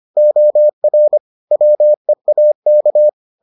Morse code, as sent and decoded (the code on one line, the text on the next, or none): --- .-. / .-- . .- -.-
OR WEAK